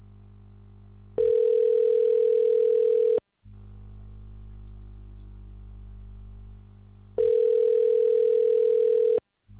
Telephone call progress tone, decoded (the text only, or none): ringback tone